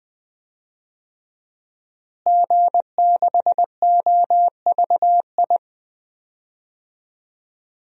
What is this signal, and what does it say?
Morse code, heard as a keyed tone, about 20 wpm, 700 Hz. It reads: G6OVI
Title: DXCC-100 Call Signs - 5 Characters - Repeated 2x - 20wpm